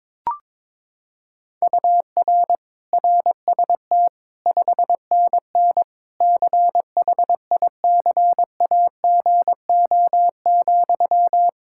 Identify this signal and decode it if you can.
Morse code, heard as a keyed tone, about 22 wpm, 700 Hz. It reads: UR RST 5NN CHICAGO,
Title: K1USN SST QSO Practice 22wpm